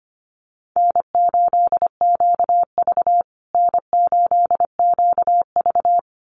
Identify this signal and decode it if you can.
Morse code, heard as a keyed tone, about 25 wpm, 700 Hz. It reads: N8QV N8QV
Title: USA Call Signs Repeated 2x 25wpm